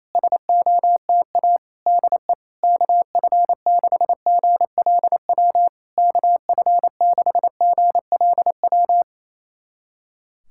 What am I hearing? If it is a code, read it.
Morse code, heard as a keyed tone, about 28 wpm, 690 Hz. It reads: SOTA DE KF6GLW KF6GLW
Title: SOTA QSO Practice 28wpm